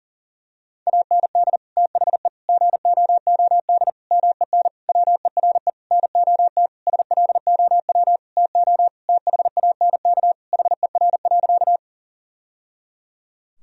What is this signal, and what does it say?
Morse code, heard as a keyed tone, about 40 wpm, 700 Hz. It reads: AND THE GOOD MEN WERE NOT SLOW TO THANK HER.
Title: Sentences from Top 1000 Words - Review Part 2 - 40wpm